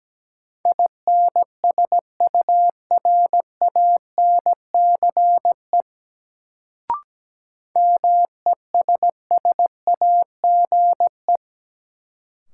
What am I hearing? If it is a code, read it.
Morse code, heard as a keyed tone, about 17 wpm, 700 Hz. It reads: INSURANCE MESSAGE
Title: Sentences from Top 1100 Words - New Word Drill Alt - 17wpm